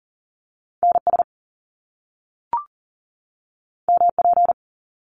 Morse code, heard as a keyed tone, about 40 wpm, 700 Hz. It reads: NS MP
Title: Top Two Letter Combinations in Words 40wpm